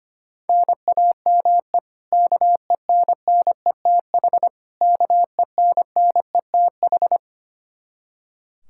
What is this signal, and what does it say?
Morse code, heard as a keyed tone, about 25 wpm, 700 Hz. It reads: NAME KENNETH KENNETH
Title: Practice QSO - Exchange 1 - Beginning - 25wpm